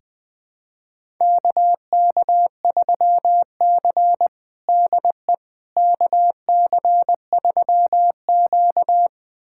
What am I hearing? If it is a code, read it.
Morse code, heard as a keyed tone, about 20 wpm, 700 Hz. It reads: KK3C DE KC3Q
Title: USA Callsign DE Callsign - 4 Characters - 20wpm